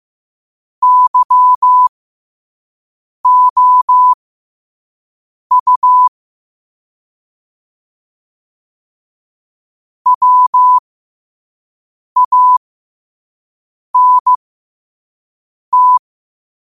Morse code, heard as a keyed tone, about 15 wpm, spaced as for 5 wpm, 1000 Hz. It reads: YOU WANT